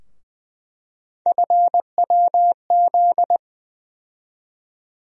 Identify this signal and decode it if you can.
Morse code, heard as a keyed tone, about 20 wpm, 700 Hz. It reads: FWZ
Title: USA Call Sign Postfixes - 3 Letters 20wpm